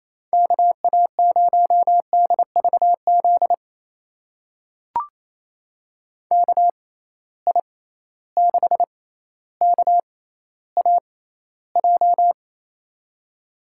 Morse code, heard as a keyed tone, about 28 wpm, spaced as for 14 wpm, 700 Hz. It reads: KA0DVZ KI6KAJ
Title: USA Call Signs - 6 Characters - 6x Character Spacing - 28wpm